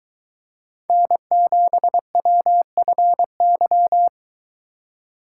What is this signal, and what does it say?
Morse code, heard as a keyed tone, about 23 wpm, 700 Hz. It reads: N7WFY